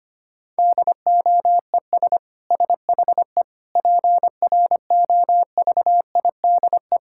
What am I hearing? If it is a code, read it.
Morse code, heard as a keyed tone, about 25 wpm, 700 Hz. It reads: DOES SHE PROVIDE
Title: Sentences from Top 1000 Words - Review Part 6 - 25wpm